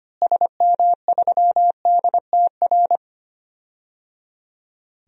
Morse code, heard as a keyed tone, about 25 wpm, 700 Hz. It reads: SM3DTR
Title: DXCC-100 Call Signs - 6 Characters - Repeated 3x - 25wpm